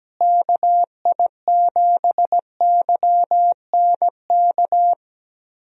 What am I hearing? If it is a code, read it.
Morse code, heard as a keyed tone, about 17 wpm, 700 Hz. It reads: KI7YNK
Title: USA Call Signs - 6 Characters - Repeated 3x - 17wpm